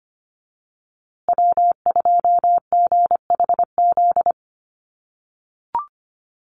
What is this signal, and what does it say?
Morse code, heard as a keyed tone, about 25 wpm, 700 Hz. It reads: W2GHZ
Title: USA Call Signs Repeated 2x 25wpm